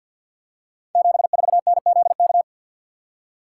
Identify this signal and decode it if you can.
Morse code, heard as a keyed tone, about 53 wpm, 700 Hz. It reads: 74NZK
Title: Code Groups - 5 Letter:Numbers - 50wpm